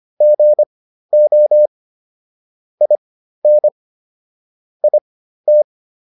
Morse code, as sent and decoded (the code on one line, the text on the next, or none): --. --- / .. -. / .. -
GO IN IT